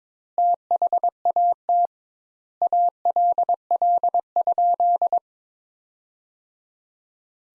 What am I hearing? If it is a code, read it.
Morse code, heard as a keyed tone, about 22 wpm, 700 Hz. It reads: THAT ALL?
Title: Sentences Top 200 Words - 22wpm